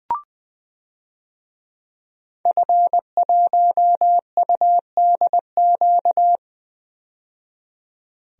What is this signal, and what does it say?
Morse code, heard as a keyed tone, about 20 wpm, 700 Hz. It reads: F1UDQ
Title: DXCC-100 Call Signs - 5 Characters - 20wpm